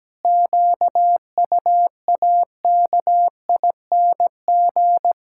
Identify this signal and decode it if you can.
Morse code, heard as a keyed tone, about 17 wpm, 700 Hz. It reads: QUAKING
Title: Pangram Challenge v2 - 17wpm